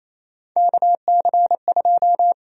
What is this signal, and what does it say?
Morse code, heard as a keyed tone, about 27 wpm, 700 Hz. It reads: KC2
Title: USA Call Sign Prefixes - Repeated 2x - 28wpm